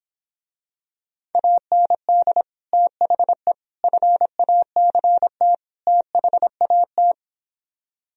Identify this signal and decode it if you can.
Morse code, heard as a keyed tone, about 26 wpm, 700 Hz. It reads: AND THE FACT THAT